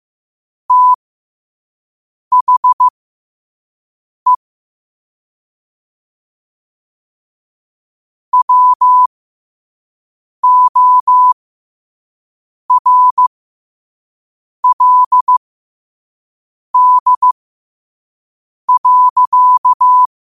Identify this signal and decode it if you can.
Morse code, heard as a keyed tone, about 15 wpm, spaced as for 5 wpm, 1000 Hz. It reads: THE WORLD.
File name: qotdfiles / 5wpm